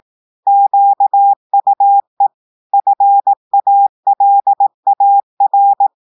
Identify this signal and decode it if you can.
Morse code, heard as a keyed tone, about 18 wpm, 800 Hz. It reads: QUE FALAR
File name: portugais texte en clair 2 vitesse 18 mots minutes